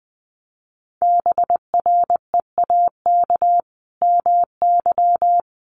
Morse code, heard as a keyed tone, about 20 wpm, 700 Hz. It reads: BREAK MY
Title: Sentences from Top 1400 Words 20wpm